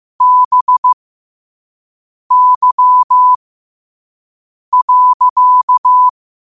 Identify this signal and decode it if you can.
Morse code, heard as a keyed tone, about 15 wpm, spaced as for 5 wpm, 1000 Hz. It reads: BY.
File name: qotdfiles / 5wpm